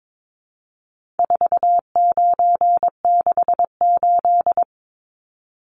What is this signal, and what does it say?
Morse code, heard as a keyed tone, about 22 wpm, 700 Hz. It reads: 4968
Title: Numbers - 4-Digit 22wpm